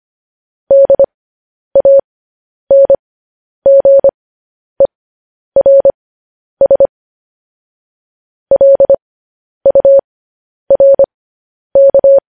Morse code, heard as a keyed tone, about 25 wpm, spaced as for 10 wpm, 550 Hz. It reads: DANGERS LURK
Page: news headlines in morse code at 10 wpm